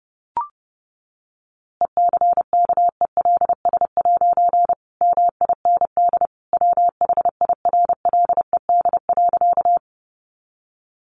Morse code, heard as a keyed tone, about 30 wpm, 700 Hz. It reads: ECKELS' MIND WHIRLED.
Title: A Sound Of Thunder 30wpm